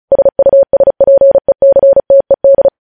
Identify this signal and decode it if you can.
Morse code, heard as a keyed tone, about 36 wpm, 550 Hz. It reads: SUSPECTED
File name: podcast / News headlines-1x-35x35